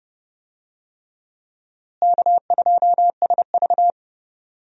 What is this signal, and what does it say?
Morse code, heard as a keyed tone, about 30 wpm, 700 Hz. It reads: K2SV
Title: USA Call Signs - 4 Characters - Repeated 3x - 30wpm